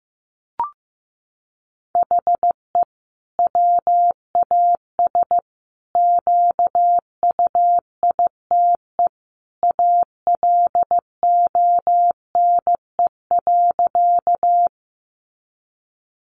Morse code, heard as a keyed tone, about 15 wpm, 700 Hz. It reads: HE WAS QUITE ALONE.
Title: Sentences from Top 1000 Words 15wpm